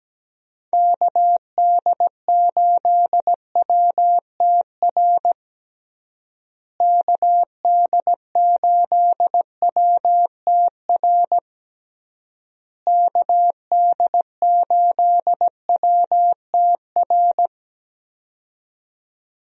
Morse code, heard as a keyed tone, about 17 wpm, 700 Hz. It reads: KD8WTR KD8WTR KD8WTR
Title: USA Call Signs - 6 Characters - Repeated 3x - 17wpm